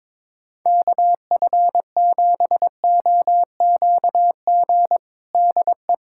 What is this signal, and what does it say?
Morse code, heard as a keyed tone, about 22 wpm, 700 Hz. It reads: KF7OQG DE
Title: USA Callsign DE Callsign - 6 Characters - 22wpm